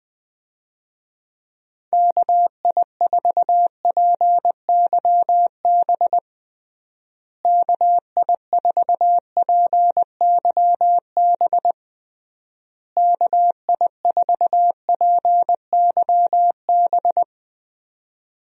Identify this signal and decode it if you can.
Morse code, heard as a keyed tone, about 21 wpm, 700 Hz. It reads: KI4PYB KI4PYB KI4PYB
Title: USA Call Signs - 6 Characters - Repeated 3x - 20wpm